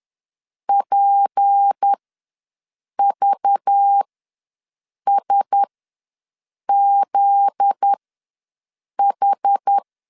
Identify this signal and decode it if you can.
Morse code, heard as a keyed tone, about 10 wpm, spaced as for 6 wpm, 800 Hz. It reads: PVSZH
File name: LEZIONI / 9 28 ws 10wpm 4124